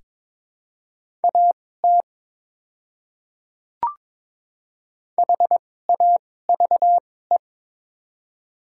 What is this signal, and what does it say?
Morse code, heard as a keyed tone, about 22 wpm, spaced as for 16 wpm, 700 Hz. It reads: AT HAVE
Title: 2x Character Spacing - Top 100 Words - 22wpm